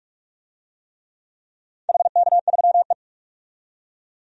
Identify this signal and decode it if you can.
Morse code, heard as a keyed tone, about 46 wpm, 700 Hz. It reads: HK3E
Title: DXCC-100 Call Signs - 4 Characters - 45wpm